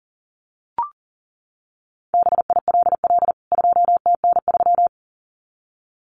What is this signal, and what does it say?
Morse code, heard as a keyed tone, about 40 wpm, 700 Hz. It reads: BILL 2TN3
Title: CWT Exchange 40wpm